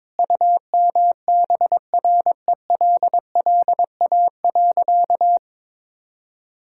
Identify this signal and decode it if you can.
Morse code, heard as a keyed tone, about 22 wpm, 700 Hz. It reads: UMBRELLA.
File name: practice / 22wpm / Sentences with 3-Word Phrases - 22wpm